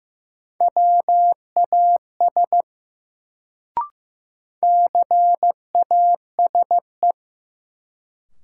Morse code, heard as a keyed tone, about 15 wpm, 700 Hz. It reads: WAS CASE
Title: Sets of 2 Words - New Word Drill - 15wpm